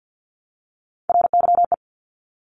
Morse code, heard as a keyed tone, about 50 wpm, 700 Hz. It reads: RYE
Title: Multi-letter Phonemes Repeated 3x 50wpm